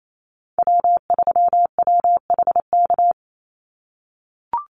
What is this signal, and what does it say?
Morse code, heard as a keyed tone, about 28 wpm, 700 Hz. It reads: W3WHK
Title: Call Signs - 28wpm